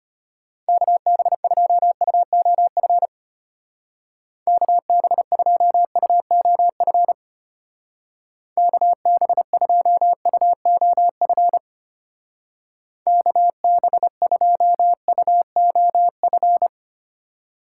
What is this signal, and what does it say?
Morse code, heard as a keyed tone, about 32 wpm, 700 Hz. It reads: KB2UOF KB2UOF KB2UOF KB2UOF